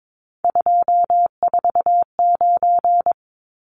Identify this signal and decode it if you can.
Morse code, heard as a keyed tone, about 22 wpm, 700 Hz. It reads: 249
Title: Numbers - 3-Digit 22wpm